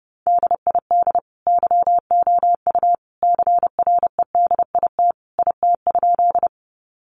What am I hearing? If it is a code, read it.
Morse code, heard as a keyed tone, about 30 wpm, 700 Hz. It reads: DID YOU CREDIT IT?
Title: Sentences from Top 1000 Words 30wpm